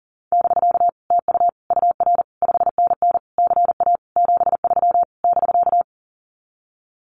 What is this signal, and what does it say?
Morse code, heard as a keyed tone, about 40 wpm, 700 Hz. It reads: <BK> TU UR 5NN CA 73 <BK>
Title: Common QSO Phrases Encore 40wpm